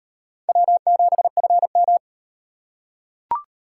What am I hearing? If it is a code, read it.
Morse code, heard as a keyed tone, about 38 wpm, 700 Hz. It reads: W7FM